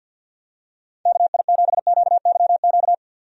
Morse code, heard as a keyed tone, about 50 wpm, 700 Hz. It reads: KI7QYQ
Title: USA Call Signs - 6 Characters - 50wpm